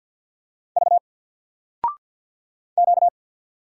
Morse code, heard as a keyed tone, about 50 wpm, 700 Hz. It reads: V Q
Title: Single Letter-Number 50wpm